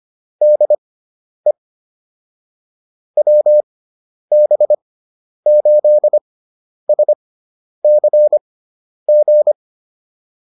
Morse code, heard as a keyed tone, about 25 wpm, spaced as for 10 wpm, 600 Hz. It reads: DE WB8SCG